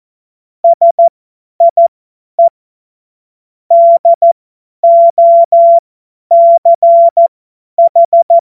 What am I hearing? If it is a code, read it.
Morse code, heard as a keyed tone, about 14 wpm, spaced as for 10 wpm, 680 Hz. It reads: SIE DOCH
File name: Morsekurs / Witze / Witz 24.Oktober 2023 50 F70